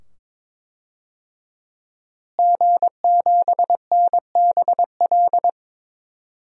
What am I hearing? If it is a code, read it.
Morse code, heard as a keyed tone, about 22 wpm, 700 Hz. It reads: G7NBL
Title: DXCC-100 Call Signs - 5 Characters - Repeated 2x - 22wpm